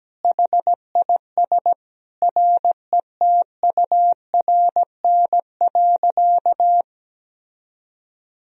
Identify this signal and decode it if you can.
Morse code, heard as a keyed tone, about 17 wpm, 700 Hz. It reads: HIS RETURN.
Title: Sentences from Top 1000 Words - Review Part 1 - 17wpm